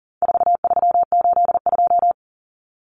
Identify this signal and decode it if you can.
Morse code, heard as a keyed tone, about 40 wpm, 700 Hz. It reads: 4382
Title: Common Cut Numbers - 4-Digit 40wpm